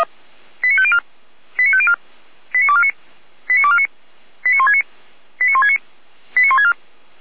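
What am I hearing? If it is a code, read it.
Five-tone selective call (ZVEI), heard as quick runs of five tones, tones of about 80 ms. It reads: 89504 89504 89249 89249 89169 89169 89164
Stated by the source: ZVEI five-tone selective call received by SDR.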